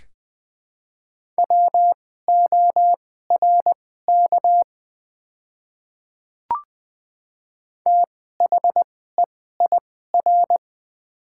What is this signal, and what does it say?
Morse code, heard as a keyed tone, about 20 wpm, spaced as for 14 wpm, 700 Hz. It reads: WORK THEIR
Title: 2x Character Spacing - Top 200 Words - 20wpm